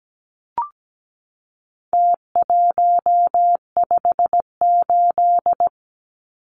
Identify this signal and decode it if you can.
Morse code, heard as a keyed tone, about 17 wpm, 700 Hz. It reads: T158
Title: Numbers - Contest Style - 4-Digit 17wpm